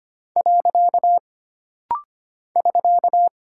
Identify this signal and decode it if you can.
Morse code, heard as a keyed tone, about 25 wpm, 700 Hz. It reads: . <SK>
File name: Single Punctuation-Prosign - Rapid-Fire - 25wpm